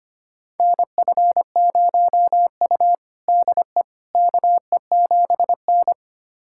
Morse code, heard as a keyed tone, about 25 wpm, 700 Hz. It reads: NF0U DE KE7N
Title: USA Callsign DE Callsign - 4 Characters - 25wpm